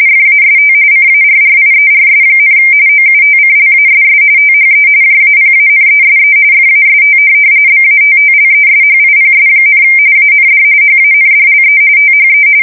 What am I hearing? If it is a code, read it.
RTTY, about 45.45 baud. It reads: FAST CARRIERS, OUTCLASS THE JAPANESE FAST BATTLESHIPS, AND HANDLE THE
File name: fsk170 / USS-New-Jersey